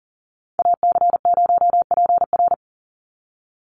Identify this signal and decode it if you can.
Morse code, heard as a keyed tone, about 40 wpm, 700 Hz. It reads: AC0PR
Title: NA SOTA Chaser Callsigns 40wpm